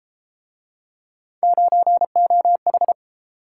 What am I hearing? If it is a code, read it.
Morse code, heard as a keyed tone, about 31 wpm, 700 Hz. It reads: 9OH